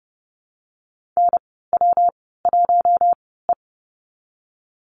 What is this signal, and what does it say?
Morse code, heard as a keyed tone, about 30 wpm, 700 Hz. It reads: N W 1 E
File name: Lesson 19g - Letter w - Call Signs 30wpm